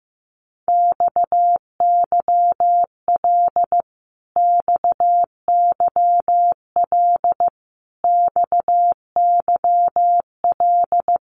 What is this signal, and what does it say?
Morse code, heard as a keyed tone, about 15 wpm, 700 Hz. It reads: XYL XYL XYL
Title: Abbreviations Repeated 3x - 15wpm